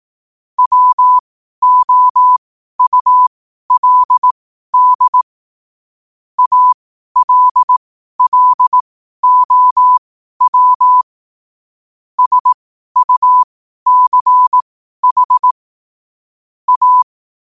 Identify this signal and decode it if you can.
Morse code, heard as a keyed tone, about 18 wpm, spaced as for 12 wpm, 1000 Hz. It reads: WOULD ALLOW SUCH A
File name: qotdfiles / 12wpm